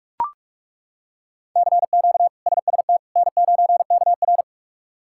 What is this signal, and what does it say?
Morse code, heard as a keyed tone, about 42 wpm, 700 Hz. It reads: CQ SST N9KR
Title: K1USN SST QSO Practice - 45wpm